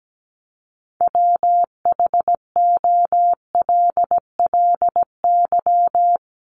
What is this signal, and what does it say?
Morse code, heard as a keyed tone, about 17 wpm, 700 Hz. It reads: WHOLLY
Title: LY Suffix - Top 100 Words with Base Words 17wpm